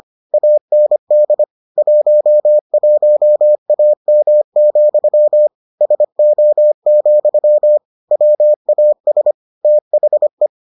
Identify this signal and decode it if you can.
Morse code, heard as a keyed tone, about 25 wpm, 600 Hz. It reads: AND 11AM, SO, WAS THE